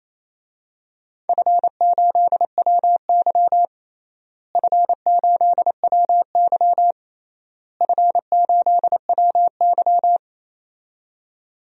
Morse code, heard as a keyed tone, about 27 wpm, 700 Hz. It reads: F8WY F8WY F8WY
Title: DXCC-100 Call Signs - 4 Characters - Repeated 3x - 28wpm